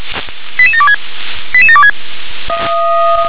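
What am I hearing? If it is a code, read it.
Five-tone selective call (ZVEI), heard as quick runs of five tones, tones of about 70 ms. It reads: 88726 88726
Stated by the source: Five-tone ZVEI selective call received on an SDR.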